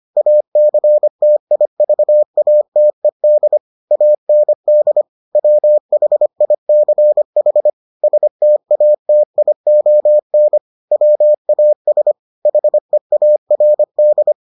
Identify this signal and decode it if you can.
Morse code, heard as a keyed tone, about 25 wpm, 600 Hz. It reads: ACTIVATED AND WHICH STATION WAS HEARD